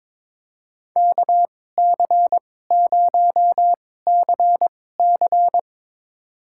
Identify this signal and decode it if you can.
Morse code, heard as a keyed tone, about 23 wpm, spaced as for 16 wpm, 700 Hz. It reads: KC0CC
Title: USA Call Signs - 5 Characters - 2x Character Spacing - 22wpm